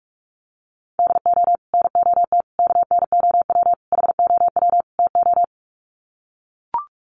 Morse code, heard as a keyed tone, about 44 wpm, 700 Hz. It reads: DO NOT KNOW HOW TO
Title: Sets of 5 Words 45wpm